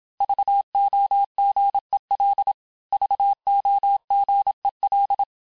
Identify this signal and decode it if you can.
Morse code, heard as a keyed tone, about 27 wpm, 800 Hz. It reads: VOGEL VOGEL
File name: cours-du-mercredi / audios / cours-869-texte-LES-MOTS-vitesse-25-MPM